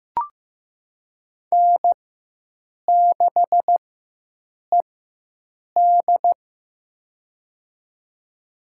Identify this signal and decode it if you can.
Morse code, heard as a keyed tone, about 15 wpm, spaced as for 7 wpm, 700 Hz. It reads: N6ED
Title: USA Call Signs - 4 Characters - 4x Character Spacing - 15wpm